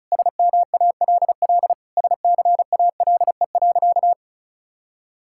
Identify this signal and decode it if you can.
Morse code, heard as a keyed tone, about 36 wpm, 700 Hz. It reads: SMALL SCALE.
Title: Sentences from Top 1000 Words - Review Part 5 - 35wpm